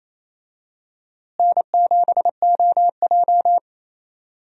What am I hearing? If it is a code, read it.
Morse code, heard as a keyed tone, about 28 wpm, 700 Hz. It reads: N7OJ
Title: USA Call Signs - 4 Characters - Repeated 2x - 28wpm